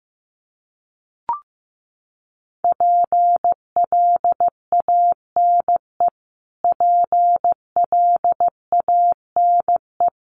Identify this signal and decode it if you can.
Morse code, heard as a keyed tone, about 15 wpm, 700 Hz. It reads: PLANE PLANE
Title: Top 500 Words Repeated 15wpm